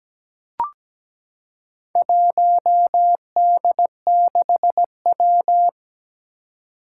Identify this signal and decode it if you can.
Morse code, heard as a keyed tone, about 17 wpm, 700 Hz. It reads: 1D6W
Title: Four Challenging Letter-Number Combinations 17wpm